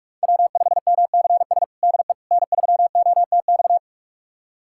Morse code, heard as a keyed tone, about 46 wpm, 700 Hz. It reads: W5MCS DE N3OTX
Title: USA Callsign DE Callsign - 5 Characters - 45wpm